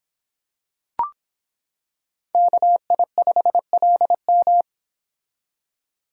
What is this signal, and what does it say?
Morse code, heard as a keyed tone, about 26 wpm, 700 Hz. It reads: KI5LM